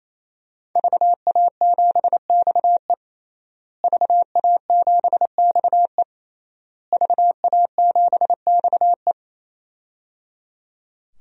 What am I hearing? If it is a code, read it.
Morse code, heard as a keyed tone, about 27 wpm, 700 Hz. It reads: VA7XE VA7XE VA7XE
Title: DXCC-100 Call Signs - 5 Characters - Repeated 3x - 28wpm